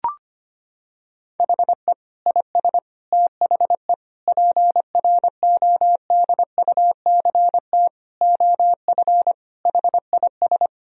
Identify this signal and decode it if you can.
Morse code, heard as a keyed tone, about 25 wpm, 700 Hz. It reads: HE IS THE PRODUCT OF HIS
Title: Sentences from Top 1000 Words - Review Part 2 - 25wpm